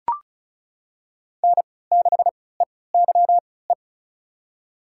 Morse code, heard as a keyed tone, about 36 wpm, spaced as for 20 wpm, 700 Hz. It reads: N6EYE
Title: USA Call Signs - 5 Characters - 3x Character Spacing - 35wpm